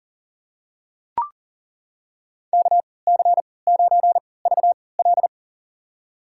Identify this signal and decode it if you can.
Morse code, heard as a keyed tone, about 40 wpm, spaced as for 23 wpm, 700 Hz. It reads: KC9VL
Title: USA Call Signs - 5 Characters - 3x Character Spacing - 40wpm